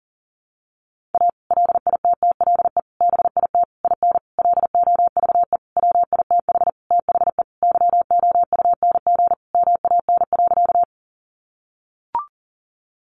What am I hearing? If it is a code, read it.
Morse code, heard as a keyed tone, about 40 wpm, 700 Hz. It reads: A LITTLE BIT IN LOVE WITH THE YOUNG MAN.
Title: Sentences from Top 900 Words 40wpm